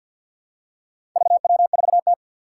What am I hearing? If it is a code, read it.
Morse code, heard as a keyed tone, about 51 wpm, 700 Hz. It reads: VW4T
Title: Four Challenging Letter-Number Combinations 50wpm